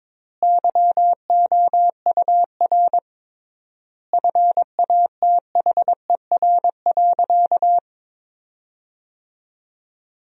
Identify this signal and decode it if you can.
Morse code, heard as a keyed tone, about 21 wpm, 700 Hz. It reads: YOUR FATHER.
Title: 3x Word-Spacing - Sentences from Top 300 Words - 22wpm